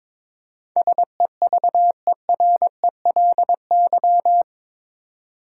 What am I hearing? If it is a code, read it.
Morse code, heard as a keyed tone, about 21 wpm, 700 Hz. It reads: SEVERELY
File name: practice / 22wpm / Eight Letter Words - 22wpm